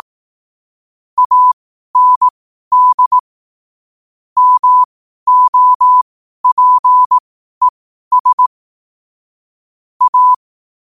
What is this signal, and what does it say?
Morse code, heard as a keyed tone, about 18 wpm, spaced as for 12 wpm, 1000 Hz. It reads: AND MOPES A
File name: qotdfiles / 12wpm